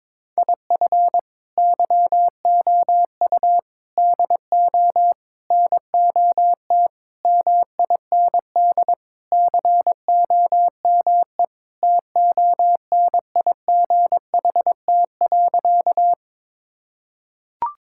Morse code, heard as a keyed tone, about 22 wpm, 700 Hz. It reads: IF YOU DO NOT MIND COME TONIGHT.
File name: practice / 22wpm / Sentences with 4-Word Phrases - 22wpm